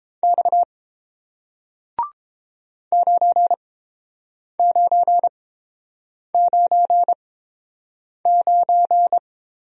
Morse code, heard as a keyed tone, about 30 wpm, spaced as for 7 wpm, 700 Hz. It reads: X 9999